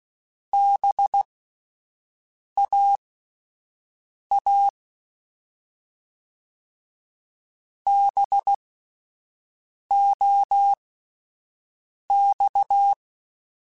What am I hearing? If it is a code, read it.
Morse code, heard as a keyed tone, about 16 wpm, spaced as for 5 wpm, 800 Hz. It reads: BAA BOX
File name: Spaced Alphas / Spaced Alphas mp3s / Bs sp @10